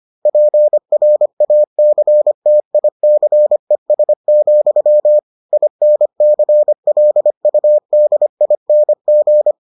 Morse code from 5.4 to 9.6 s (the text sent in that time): INCLUDING